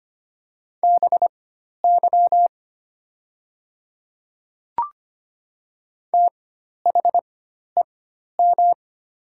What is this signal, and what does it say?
Morse code, heard as a keyed tone, about 25 wpm, spaced as for 12 wpm, 700 Hz. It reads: BY THEM
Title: 4x Character Spacing - Top 200 Words - 25wpm